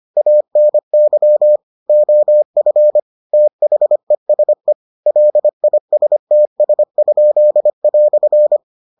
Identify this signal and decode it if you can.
Morse code, heard as a keyed tone, about 25 wpm, 600 Hz. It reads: ANY OF THESE LISTS?"